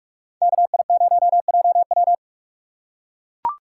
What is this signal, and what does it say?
Morse code, heard as a keyed tone, about 44 wpm, 700 Hz. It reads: KI0JW